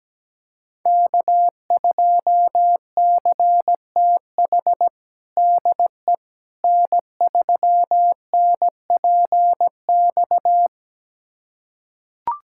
Morse code, heard as a keyed tone, about 17 wpm, 700 Hz. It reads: K2CTH DE N3NPX
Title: USA Callsign DE Callsign - 5 Characters - 17wpm